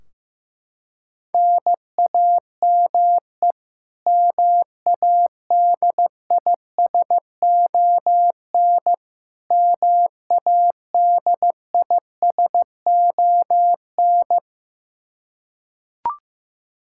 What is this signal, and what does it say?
Morse code, heard as a keyed tone, about 15 wpm, 700 Hz. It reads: NAME MADISON MADISON
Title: K1USN SST QSO Practice 15wpm